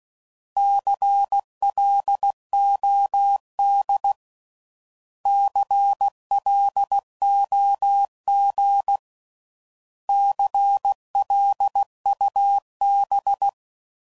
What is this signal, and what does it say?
Morse code, heard as a keyed tone, about 16 wpm, 800 Hz. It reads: CLOD CLOG CLUB
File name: Prefixes / By First Letter / Cs